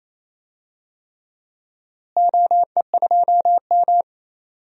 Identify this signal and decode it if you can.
Morse code, heard as a keyed tone, about 28 wpm, 700 Hz. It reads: OE2M
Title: DXCC-100 Call Signs - 4 Characters - Repeated 3x - 28wpm